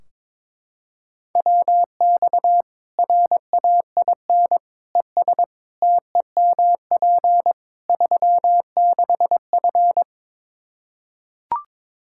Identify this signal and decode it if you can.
Morse code, heard as a keyed tone, about 21 wpm, 700 Hz. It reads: WX RAIN ES TEMP 36F
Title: Practice QSO - Exchange 2 - Info - 22wpm